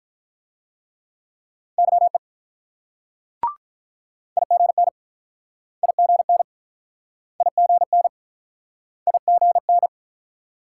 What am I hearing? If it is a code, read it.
Morse code, heard as a keyed tone, about 48 wpm, 700 Hz. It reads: YE IGN IGN IGN IGN